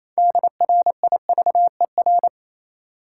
Morse code, heard as a keyed tone, about 28 wpm, 700 Hz. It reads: DRIVER
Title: Sentences from Top 1000 Words - New Word Drill - 28wpm